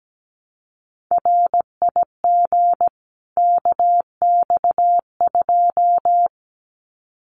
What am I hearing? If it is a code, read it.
Morse code, heard as a keyed tone, about 17 wpm, 700 Hz. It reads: RIG KX2
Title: Common QSO Phrases 17wpm